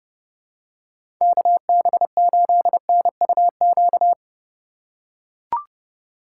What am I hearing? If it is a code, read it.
Morse code, heard as a keyed tone, about 30 wpm, 700 Hz. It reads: KB8NUQ
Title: USA Call Signs - 6 Characters - 3x Character Spacing - 30wpm